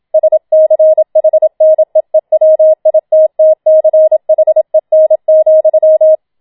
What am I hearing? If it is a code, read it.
Morse code, heard as a keyed tone, about 27 wpm, 630 Hz. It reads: SCHNEEWITTCHEN,
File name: Schneewittchen Teil2 27wpm